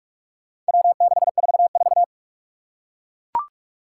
Morse code, heard as a keyed tone, about 41 wpm, 700 Hz. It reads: W644
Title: Four Challenging Letter-Number Combinations 45wpm